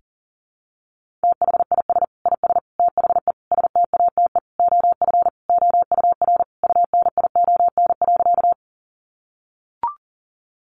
Morse code, heard as a keyed tone, about 40 wpm, 700 Hz. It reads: THIS IS THE STATE OF OUR UNION.
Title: Sentences from Top 1400 Words 40wpm